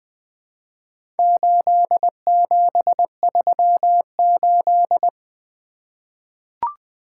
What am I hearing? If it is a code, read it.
Morse code, heard as a keyed tone, about 20 wpm, 700 Hz. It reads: 8738
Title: Four Letter-Number Combinations - 20wpm